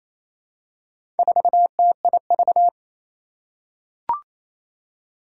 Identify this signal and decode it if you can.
Morse code, heard as a keyed tone, about 27 wpm, 700 Hz. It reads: 4TIV
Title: Four Challenging Letter-Number Combinations 28wpm